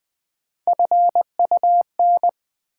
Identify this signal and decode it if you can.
Morse code, heard as a keyed tone, about 20 wpm, 700 Hz. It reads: FUN
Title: Sentences from Top 700 Words - New Word Drill - 20wpm